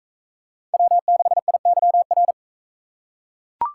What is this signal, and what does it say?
Morse code, heard as a keyed tone, about 41 wpm, 700 Hz. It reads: W6IYR